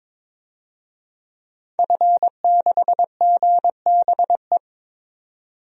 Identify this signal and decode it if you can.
Morse code, heard as a keyed tone, about 22 wpm, 700 Hz. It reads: F6GBE
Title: DXCC-100 Call Signs - 5 Characters - 22wpm